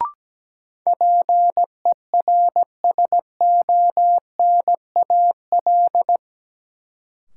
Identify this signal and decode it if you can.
Morse code, heard as a keyed tone, about 17 wpm, 700 Hz. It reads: PERSONAL
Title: Sentences from Top 700 Words - New Word Drill - 17wpm